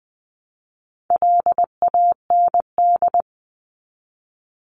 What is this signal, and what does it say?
Morse code, heard as a keyed tone, about 20 wpm, 700 Hz. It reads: LAND
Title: Top 500 Words 20wpm